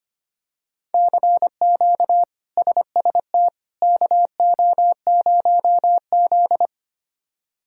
Morse code, heard as a keyed tone, about 25 wpm, 700 Hz. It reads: CQ SST KO0Z
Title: K1USN SST QSO Practice - 25wpm